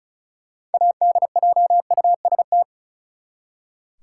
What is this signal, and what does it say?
Morse code, heard as a keyed tone, about 36 wpm, 700 Hz. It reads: ADJUST
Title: Six Letter Words - 35wpm